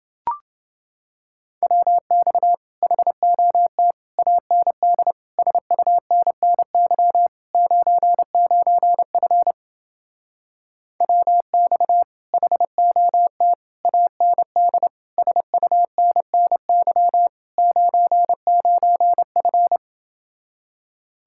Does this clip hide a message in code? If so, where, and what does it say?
Yes: Morse code from 1.6 to 19.8 s, heard as a keyed tone, about 29 wpm, 700 Hz. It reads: WX HOT AND SUNNY 99F WX HOT AND SUNNY 99F